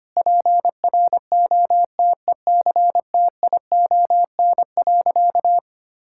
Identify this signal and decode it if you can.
Morse code, heard as a keyed tone, about 25 wpm, 700 Hz. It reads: PROTECTION.
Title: Sentences from Top 1600 Words 25wpm